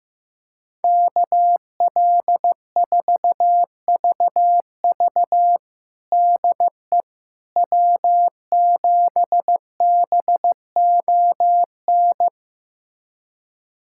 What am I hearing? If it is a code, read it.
Morse code, heard as a keyed tone, about 15 wpm, 700 Hz. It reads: KL4VV DE W7BON
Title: USA Callsign DE Callsign - 5 Characters - 15wpm